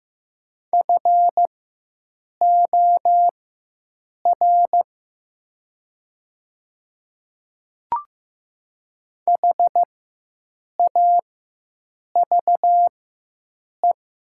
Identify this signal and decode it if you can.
Morse code, heard as a keyed tone, about 15 wpm, spaced as for 7 wpm, 700 Hz. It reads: FOR HAVE
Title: 4x Character Spacing - Top 300 Words - 15wpm